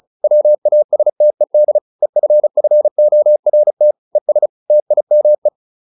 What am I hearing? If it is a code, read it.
Morse code, heard as a keyed tone, about 34 wpm, 600 Hz. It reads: WASTED EFFORT ES TIME